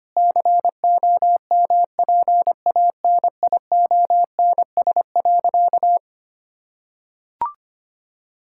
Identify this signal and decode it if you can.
Morse code, heard as a keyed tone, about 25 wpm, 700 Hz. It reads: COMPANIONS.
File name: practice / 25wpm / Difficult Sentences with Hints - 25wpm